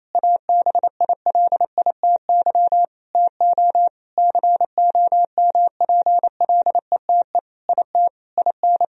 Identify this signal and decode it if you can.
Morse code, heard as a keyed tone, about 27 wpm, 700 Hz. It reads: ABILITY TO COMPLETE IT IN
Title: Sentences from Top 1000 Words - Review Part 3 - 28wpm